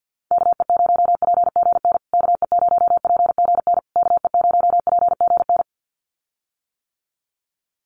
Morse code, heard as a keyed tone, about 49 wpm, 700 Hz. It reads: KE0PGN KE0PGN KE0PGN
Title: USA Call Signs Repeated 3x 50wpm